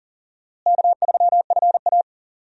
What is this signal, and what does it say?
Morse code, heard as a keyed tone, about 40 wpm, 700 Hz. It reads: K3FA
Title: USA Call Signs - 4 Characters - Repeated 3x - 40wpm